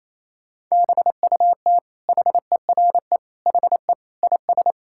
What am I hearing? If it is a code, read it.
Morse code, heard as a keyed tone, about 28 wpm, 700 Hz. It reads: BUT HERE HE IS
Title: Sentences from Top 1600 Words 28wpm